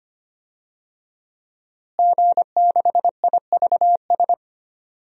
Morse code, heard as a keyed tone, about 25 wpm, 700 Hz. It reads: G6IVS
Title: DXCC-100 Call Signs - 5 Characters - Repeated 3x - 25wpm